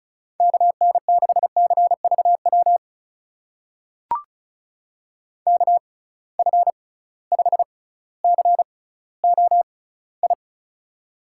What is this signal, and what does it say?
Morse code, heard as a keyed tone, about 36 wpm, spaced as for 18 wpm, 700 Hz. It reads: KN6CVW KF5COI
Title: USA Call Signs - 6 Characters - 6x Character Spacing - 35wpm